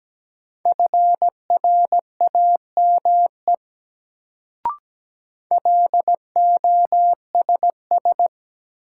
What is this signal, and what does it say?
Morse code, heard as a keyed tone, about 17 wpm, 700 Hz. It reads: FRAME LOSS
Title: Sentences from Top 1400 Words - New Word Drill Alt - 17wpm